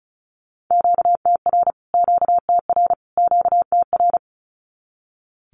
Morse code, heard as a keyed tone, about 34 wpm, 700 Hz. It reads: QTR QTR QTR
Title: Q-Codes Repeated 3x 35wpm